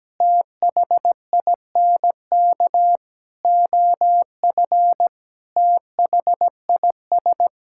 Morse code, heard as a keyed tone, about 17 wpm, 700 Hz. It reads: THINK OF THIS
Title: Sentences from Top 1600 Words 17wpm